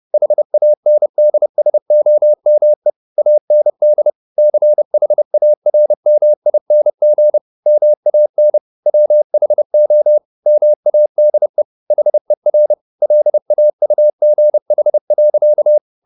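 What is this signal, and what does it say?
Morse code, heard as a keyed tone, about 30 wpm, 600 Hz. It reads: HANDSOME AND CHARMING MAN WHO MADE HER LAUGH.
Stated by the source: Morse code at 30 wpm, the novel Never by Ken Follett in English, from CW med Gnister.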